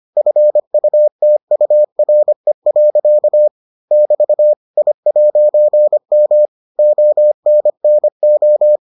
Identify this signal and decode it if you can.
Morse code, heard as a keyed tone, about 25 wpm, 600 Hz. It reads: FUTURE. <BT> I'M ONNO